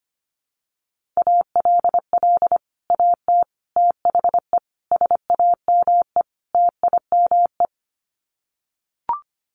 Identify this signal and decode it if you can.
Morse code, heard as a keyed tone, about 25 wpm, 700 Hz. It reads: ALL AT THE SAME TIME
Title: Sets of 5 Words Encore2 25wpm